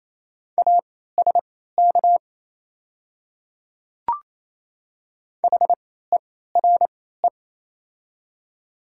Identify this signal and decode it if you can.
Morse code, heard as a keyed tone, about 28 wpm, spaced as for 16 wpm, 700 Hz. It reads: ASK HERE
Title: 3x Character Spacing - Top 200 Words - 28wpm